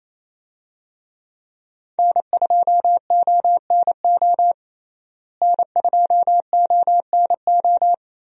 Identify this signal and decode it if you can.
Morse code, heard as a keyed tone, about 28 wpm, 700 Hz. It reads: N2ONO N2ONO
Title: USA Call Signs - 5 Characters - Repeated 3x - 28wpm